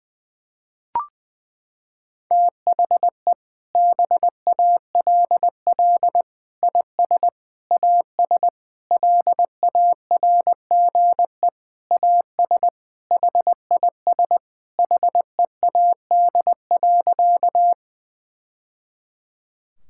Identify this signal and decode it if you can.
Morse code, heard as a keyed tone, about 20 wpm, 700 Hz. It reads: THE BALL IS AS LARGE AS HIS HEAD.
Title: Sentences from Top 1000 Words - Review Part 6 - 20wpm